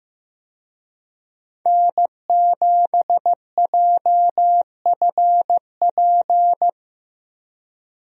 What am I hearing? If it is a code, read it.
Morse code, heard as a keyed tone, about 15 wpm, 700 Hz. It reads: N7JFP
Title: USA Call Signs - 5 Characters - Repeated 3x - 15wpm